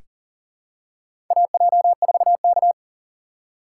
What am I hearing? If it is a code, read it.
Morse code, heard as a keyed tone, about 40 wpm, 700 Hz. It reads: AJ4K
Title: USA Call Signs - 4 Characters - 3x Character Spacing - 40wpm